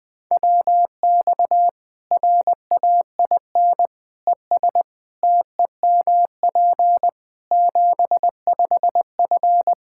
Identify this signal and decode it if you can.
Morse code, heard as a keyed tone, about 20 wpm, 700 Hz. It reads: WX RAIN ES TEMP 75F
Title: Practice QSO - Exchange 2 - Info - 20wpm